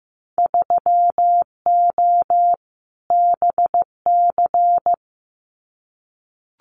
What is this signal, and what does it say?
Morse code, heard as a keyed tone, about 15 wpm, 700 Hz. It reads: 3O BC
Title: Winter Field Day Exchanges Encore 15wpm